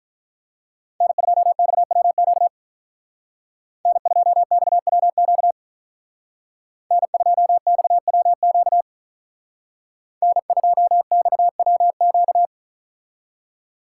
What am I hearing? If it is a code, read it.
Morse code, heard as a keyed tone, about 43 wpm, 700 Hz. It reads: N2XWQ N2XWQ N2XWQ N2XWQ
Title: Speed Racing - USA 1x3 Call Signs 35wpm